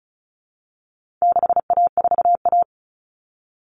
Morse code, heard as a keyed tone, about 34 wpm, 700 Hz. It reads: 6A4A
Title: Common Cut Numbers - 4-Digit 35wpm